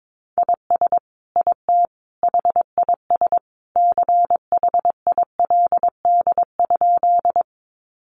Morse code, heard as a keyed tone, about 22 wpm, 700 Hz. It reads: IS IT HIS CHILD?
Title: Sentences from Top 800 Words 22wpm